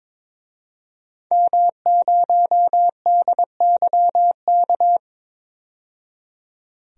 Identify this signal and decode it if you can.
Morse code, heard as a keyed tone, about 22 wpm, 700 Hz. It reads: M0DYK
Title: DXCC-100 Call Signs - 5 Characters - 22wpm